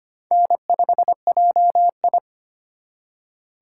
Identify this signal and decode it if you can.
Morse code, heard as a keyed tone, about 33 wpm, 700 Hz. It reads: N5JI